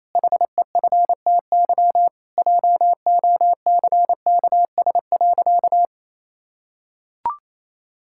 Morse code, heard as a keyed tone, about 28 wpm, 700 Hz. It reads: HEFTY JOCKS.
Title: Pangram Challenge v2 - 28wpm